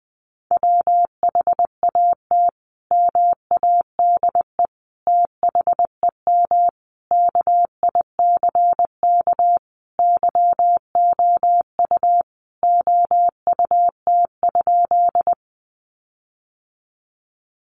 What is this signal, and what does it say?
Morse code, heard as a keyed tone, about 20 wpm, 700 Hz. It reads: WHAT MADE THEM KICK YOU OUT?
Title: Sentences from Top 1300 Words 20wpm